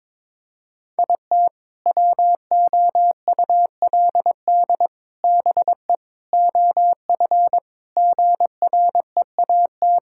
Morse code, heard as a keyed tone, about 21 wpm, 700 Hz. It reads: IT WOULD BE OF GREAT
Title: Sentences from Top 1000 Words - Review Part 6 - 22wpm